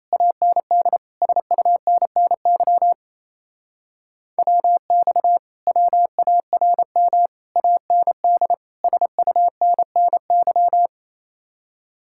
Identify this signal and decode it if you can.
Morse code, heard as a keyed tone, about 37 wpm, 700 Hz. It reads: AND SUNNY WX WARM AND SUNNY